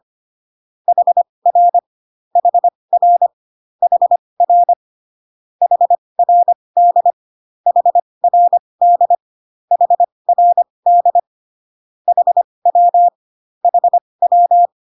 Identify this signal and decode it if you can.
Morse code, heard as a keyed tone, about 25 wpm, spaced as for 20 wpm, 700 Hz. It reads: HR HR HR HRD HRD HRD HW HW